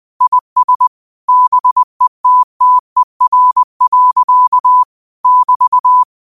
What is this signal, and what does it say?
Morse code, heard as a keyed tone, about 20 wpm, 1000 Hz. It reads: IS BETTER. <BT>
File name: qotdfiles / 20wpm